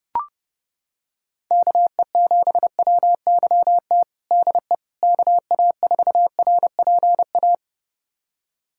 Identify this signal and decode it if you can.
Morse code, heard as a keyed tone, about 30 wpm, 700 Hz. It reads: KE7WYT DE KA4RPA
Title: USA Callsign DE Callsign 30wpm